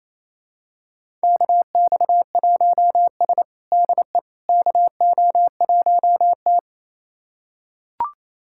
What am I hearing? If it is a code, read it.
Morse code, heard as a keyed tone, about 27 wpm, 700 Hz. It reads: KX1S DE KO1T
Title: USA Callsign DE Callsign - 4 Characters - 28wpm